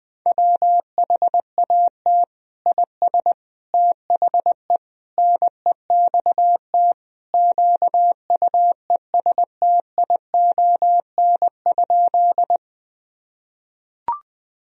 Morse code, heard as a keyed tone, about 21 wpm, 700 Hz. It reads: WHAT IS THE NEXT QUESTION?
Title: Sentences from Top 1000 Words - Review Part 1 - 20wpm